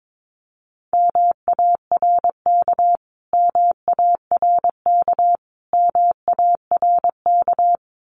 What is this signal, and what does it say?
Morse code, heard as a keyed tone, about 22 wpm, 700 Hz. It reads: MARK MARK MARK
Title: Top 300 Words Repeated 22wpm